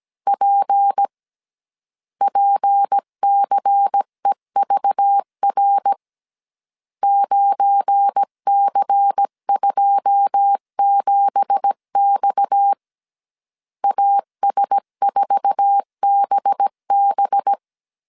Morse code, heard as a keyed tone, about 17 wpm, 800 Hz. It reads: P PCEVR 9C27X AS4BB
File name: CW / DET / 1 9 1500C S B 16wpm 2856